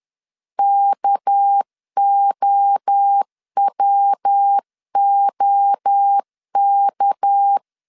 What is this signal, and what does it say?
Morse code, heard as a keyed tone, about 10 wpm, 800 Hz. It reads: KOWOK
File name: CW / DET / 1 6 1500C S B 10wpm 4137